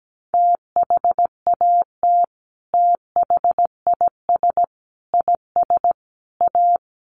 Morse code, heard as a keyed tone, about 17 wpm, 700 Hz. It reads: THAT THIS IS A
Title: Sets of 4 Words 17wpm